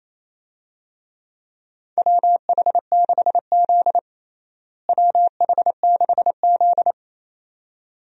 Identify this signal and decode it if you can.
Morse code, heard as a keyed tone, about 28 wpm, 700 Hz. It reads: WH6Z WH6Z
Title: USA Call Signs - 4 Characters - Repeated 2x - 28wpm